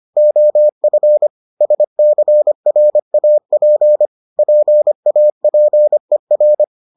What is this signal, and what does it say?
Morse code, heard as a keyed tone, about 25 wpm, 600 Hz. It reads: OF SCRAP PAPER